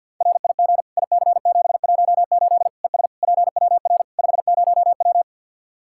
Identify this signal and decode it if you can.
Morse code, heard as a keyed tone, about 47 wpm, 700 Hz. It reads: RIG IC718 ES PWR 50W
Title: Practice QSO - Exchange 2 - Info - 50wpm